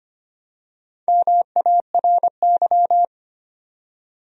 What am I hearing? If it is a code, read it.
Morse code, heard as a keyed tone, about 25 wpm, 700 Hz. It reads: MARY
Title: USA Common Names - Max 5 Letters - 25wpm